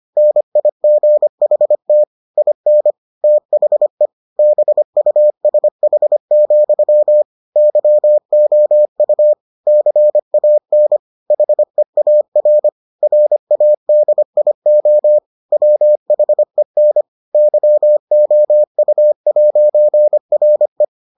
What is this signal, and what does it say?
Morse code, heard as a keyed tone, about 25 wpm, 600 Hz. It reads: NIGHT IN THE BUSH, YOU CAN HEAR RADIO WHEN YOU'RE